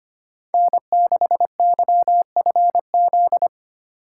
Morse code, heard as a keyed tone, about 25 wpm, 700 Hz. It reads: N6YFZ
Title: USA Call Signs - 2x Character Spacing - 25wpm